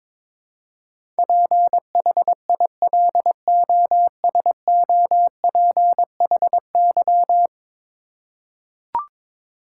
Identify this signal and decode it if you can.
Morse code, heard as a keyed tone, about 21 wpm, 700 Hz. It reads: PHILOSOPHY
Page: morse-code-ninja-practice-22wpm